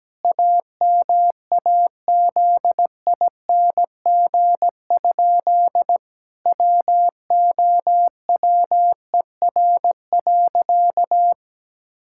Morse code, heard as a keyed tone, about 17 wpm, 700 Hz. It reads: AMAZING? WOWER.
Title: Pangram Challenge v2 - 17wpm